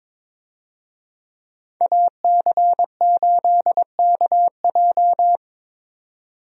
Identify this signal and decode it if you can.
Morse code, heard as a keyed tone, about 22 wpm, 700 Hz. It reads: AC8KJ
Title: USA Call Signs - 5 Characters - 22wpm